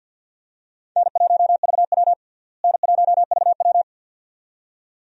Morse code, heard as a keyed tone, about 53 wpm, 700 Hz. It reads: N1VW N1VW
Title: USA Call Signs - 4 Characters - Repeated 2x - 50wpm